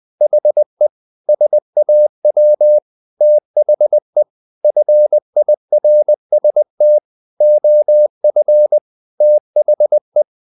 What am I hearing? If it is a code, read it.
Morse code, heard as a keyed tone, about 20 wpm, 600 Hz. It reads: HE SAW THE FIRST OF THE
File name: the old man and the sea part 2 20wpm